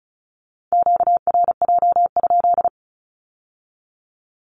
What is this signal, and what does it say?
Morse code, heard as a keyed tone, about 36 wpm, 700 Hz. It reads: QRJ?
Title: Q-Codes with Spoken Meaning - 35wpm